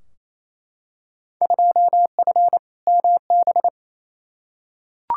Morse code, heard as a keyed tone, about 28 wpm, 700 Hz. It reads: 2F MB
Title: Field Day Exchanges Encore - 28wpm